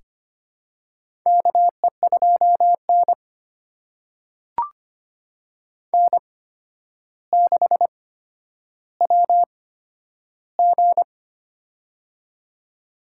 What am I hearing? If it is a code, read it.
Morse code, heard as a keyed tone, about 25 wpm, spaced as for 11 wpm, 700 Hz. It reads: KE2N N6WG
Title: USA Call Signs - 4 Characters - 8x Character Spacing - 25wpm